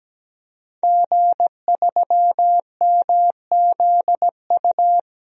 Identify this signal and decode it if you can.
Morse code, heard as a keyed tone, about 17 wpm, 700 Hz. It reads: G3MZU
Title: DXCC-100 Call Signs - 5 Characters - 17wpm